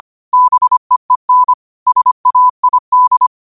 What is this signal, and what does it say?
Morse code, heard as a keyed tone, about 25 wpm, 1000 Hz. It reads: BEEN SAID
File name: qotdfiles / 25wpm